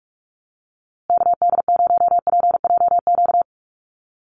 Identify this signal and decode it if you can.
Morse code, heard as a keyed tone, about 44 wpm, 700 Hz. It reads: KD0PJQ
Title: USA Call Signs Repeated 2x 45wpm